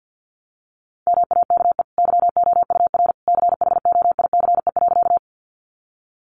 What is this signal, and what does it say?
Morse code, heard as a keyed tone, about 50 wpm, 700 Hz. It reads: MAKE YOUR CHOICE.
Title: Sentences from Top 900 Words 50wpm